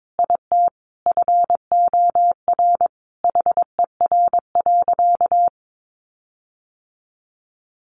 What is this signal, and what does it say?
Morse code, heard as a keyed tone, about 22 wpm, 700 Hz. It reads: IT FOR HER.
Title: Sentences from Top 1300 Words 22wpm